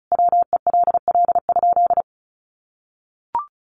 Morse code, heard as a keyed tone, about 34 wpm, 700 Hz. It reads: WELL?
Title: Sentences from Top 1400 Words 35wpm